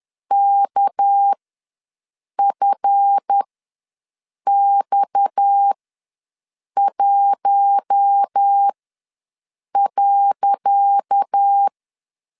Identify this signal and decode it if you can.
Morse code, heard as a keyed tone, about 10 wpm, spaced as for 6 wpm, 800 Hz. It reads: KFX1.
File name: LEZIONI / 10 SIGNS / 10C DOT ws 10wpm 4213